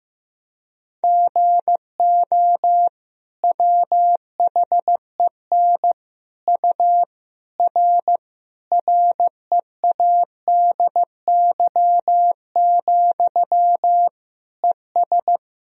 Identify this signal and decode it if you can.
Morse code, heard as a keyed tone, about 16 wpm, 700 Hz. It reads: GO WHEN U R READY, ES
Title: Sentences from Top 500 Words and Abbreviations 15wpm